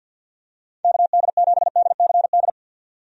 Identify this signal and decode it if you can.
Morse code, heard as a keyed tone, about 47 wpm, 700 Hz. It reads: KD7DCD
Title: USA Call Signs - 6 Characters - 50wpm